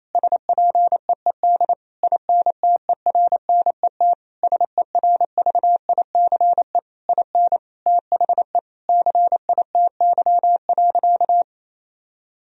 Morse code, heard as a keyed tone, about 28 wpm, 700 Hz. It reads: SPEED INTERNET SERVICE IN THE CITY.
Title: Sentences from Top 1000 Words - Review Part 4 - 28wpm